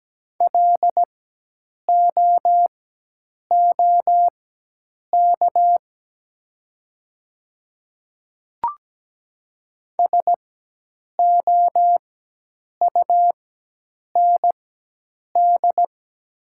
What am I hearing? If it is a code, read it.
Morse code, heard as a keyed tone, about 17 wpm, spaced as for 8 wpm, 700 Hz. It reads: LOOK SOUND
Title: 4x Character Spacing - Top 100 Words - 17wpm